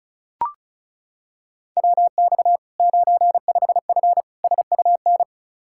Morse code, heard as a keyed tone, about 36 wpm, 700 Hz. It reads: WX 95F SUN
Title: Practice QSO - Exchange 2 - Info - 35wpm